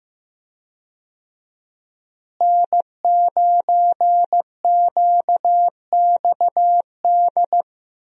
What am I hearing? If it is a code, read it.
Morse code, heard as a keyed tone, about 15 wpm, 700 Hz. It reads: N9QXD
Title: USA Call Signs - 5 Characters - Repeated 2x - 15wpm